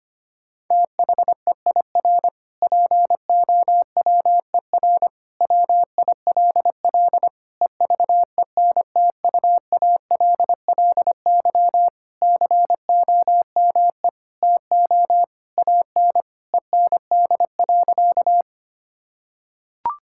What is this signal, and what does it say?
Morse code, heard as a keyed tone, about 25 wpm, 700 Hz. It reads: THEIR POWER WILL EVENTUALLY COME TO AN END.
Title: Sentences from Top 1000 Words - Review Part 6 - 25wpm